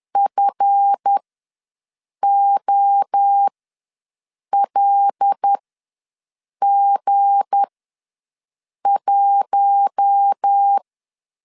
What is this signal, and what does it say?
Morse code, heard as a keyed tone, about 10 wpm, spaced as for 6 wpm, 800 Hz. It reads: FOLG1
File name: LEZIONI / 7 GF9Z / 7 GF9 ws 10wpm 4034